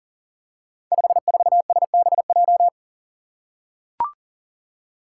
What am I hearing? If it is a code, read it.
Morse code, heard as a keyed tone, about 42 wpm, 700 Hz. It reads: 54SBJ
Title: Five Challenging Letter-Number Combinations 40wpm